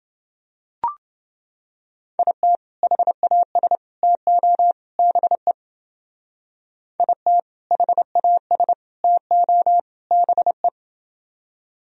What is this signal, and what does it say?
Morse code, heard as a keyed tone, about 29 wpm, 700 Hz. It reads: IT HAS TO BE IT HAS TO BE